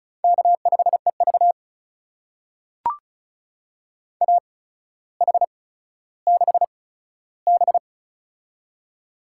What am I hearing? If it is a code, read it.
Morse code, heard as a keyed tone, about 36 wpm, spaced as for 15 wpm, 700 Hz. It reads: K5EV AH6B
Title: USA Call Signs - 4 Characters - 8x Character Spacing - 35wpm